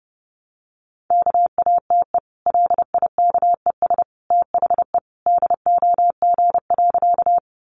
Morse code, heard as a keyed tone, about 30 wpm, 700 Hz. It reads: KATE LIKES THE DOG.
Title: Easy Phrases 30wpm